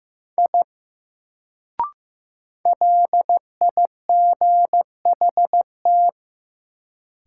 Morse code, heard as a keyed tone, about 15 wpm, 700 Hz. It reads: I LIGHT
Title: Sentences from Top 200 Words - New Word Drill - 15wpm